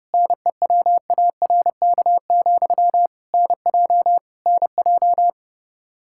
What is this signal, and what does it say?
Morse code, heard as a keyed tone, about 32 wpm, 700 Hz. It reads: NEWARK, NJ NJ
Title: Practice QSO - Exchange 1 - Beginning - 30wpm